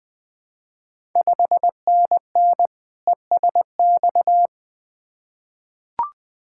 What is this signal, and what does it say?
Morse code, heard as a keyed tone, about 20 wpm, 700 Hz. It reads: 5NN ESX
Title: Ontario QSO Party Exchange 20wpm